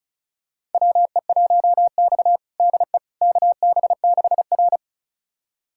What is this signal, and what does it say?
Morse code, heard as a keyed tone, about 36 wpm, 700 Hz. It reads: WE1X DE KB6R
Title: USA Callsign DE Callsign - 4 Characters - 35wpm